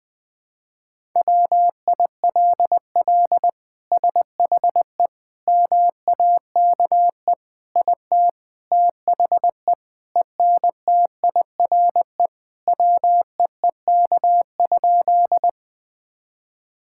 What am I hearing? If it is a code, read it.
Morse code, heard as a keyed tone, about 20 wpm, 700 Hz. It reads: WILL SHE MAKE IT THE ENTIRE WEEK?
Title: Sentences from Top 1000 Words - Review Part 5 - 20wpm